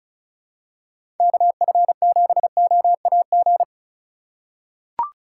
Morse code, heard as a keyed tone, about 36 wpm, 700 Hz. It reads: KF7OAG
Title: USA Call Signs - 6 Characters - 3x Character Spacing - 35wpm